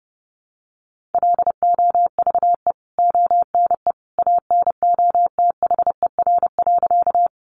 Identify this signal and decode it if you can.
Morse code, heard as a keyed tone, about 30 wpm, 700 Hz. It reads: LOVE ONE ANOTHER.
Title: Sentences from Top 600 Words 30wpm